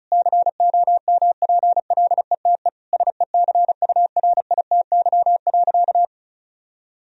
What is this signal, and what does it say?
Morse code, heard as a keyed tone, about 36 wpm, 700 Hz. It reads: COMPLETE SECURITY.
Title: Sentences from Top 1000 Words - Review Part 4 - 35wpm